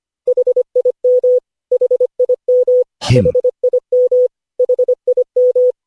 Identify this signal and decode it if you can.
Morse code, heard as a keyed tone, about 25 wpm, 500 Hz. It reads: HIM HIM HIM HIM
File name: cw / Session5 cw training